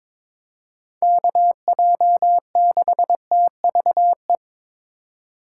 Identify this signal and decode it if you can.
Morse code, heard as a keyed tone, about 22 wpm, 700 Hz. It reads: KJ6TVE
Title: USA Call Signs - 6 Characters - 2x Character Spacing - 22wpm